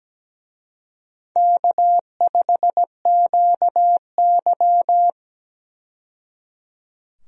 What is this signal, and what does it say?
Morse code, heard as a keyed tone, about 17 wpm, 700 Hz. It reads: K5QY
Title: USA Call Signs - 4 Characters - 17wpm